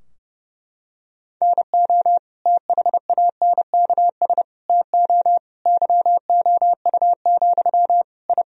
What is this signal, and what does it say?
Morse code, heard as a keyed tone, about 30 wpm, 700 Hz. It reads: NO THANKS TO YOU, I
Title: Sentences from Top 1000 Words - Review Part 3 - 30wpm